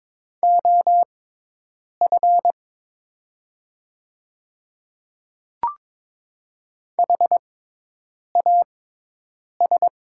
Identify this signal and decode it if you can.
Morse code, heard as a keyed tone, about 22 wpm, spaced as for 8 wpm, 700 Hz. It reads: OF HAS
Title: 6x Character Spacing - Top 100 Words - 22wpm